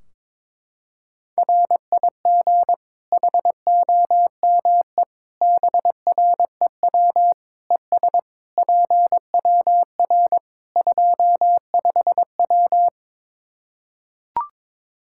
Morse code, heard as a keyed tone, about 22 wpm, 700 Hz. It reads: RIG HOME BREW ES PWR 25W
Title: Practice QSO - Exchange 2 - Info - 22wpm